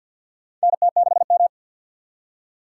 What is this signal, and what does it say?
Morse code, heard as a keyed tone, about 52 wpm, 700 Hz. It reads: NT6M
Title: USA Call Signs - 4 Characters - 6x Character Spacing - 50wpm